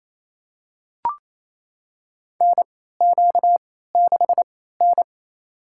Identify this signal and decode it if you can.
Morse code, heard as a keyed tone, about 28 wpm, spaced as for 16 wpm, 700 Hz. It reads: NQ6N
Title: USA Call Signs - 4 Characters - 3x Character Spacing - 28wpm